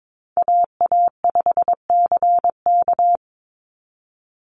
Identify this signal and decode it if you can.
Morse code, heard as a keyed tone, about 22 wpm, 700 Hz. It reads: AA5CK
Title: NA SOTA Chaser Callsigns 22wpm